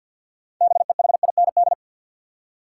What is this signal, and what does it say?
Morse code, heard as a keyed tone, about 53 wpm, 700 Hz. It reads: BEHIND
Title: Six Letter Words - 50wpm